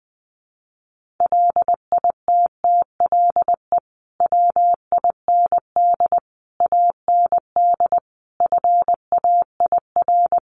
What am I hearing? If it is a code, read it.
Morse code, heard as a keyed tone, about 20 wpm, 700 Hz. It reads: LITTLE WIND AND FAIR
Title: Sentences from Top 1300 Words 20wpm